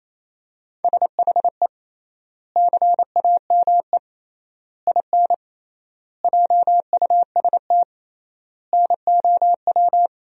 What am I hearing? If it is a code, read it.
Morse code, heard as a keyed tone, about 28 wpm, 700 Hz. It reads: SHE CAME IN JUST NOW
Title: 3x Word-Spacing - Sentences from Top 300 Words - 28wpm